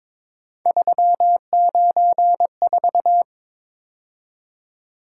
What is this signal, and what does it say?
Morse code, heard as a keyed tone, about 22 wpm, 700 Hz. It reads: 394
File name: practice / 22wpm / Three Letter-Number Combinations - 22wpm